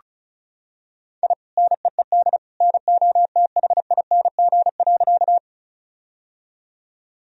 Morse code, heard as a keyed tone, about 36 wpm, 700 Hz. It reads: I NEED NOTHING.
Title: Sentences from Top 1000 Words - Review Part 1 - 35wpm